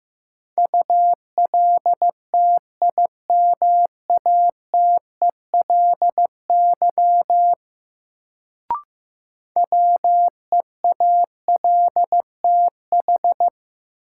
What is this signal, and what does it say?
Morse code, heard as a keyed tone, about 16 wpm, 700 Hz. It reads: ULTIMATELY WEALTH
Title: Sentences from Top 1200 Words - New Word Drill Alt - 15wpm